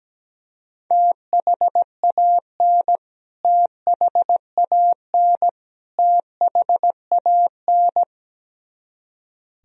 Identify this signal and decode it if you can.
Morse code, heard as a keyed tone, about 17 wpm, 700 Hz. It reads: THAN THAN THAN
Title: Top 100 Words Repeated 17wpm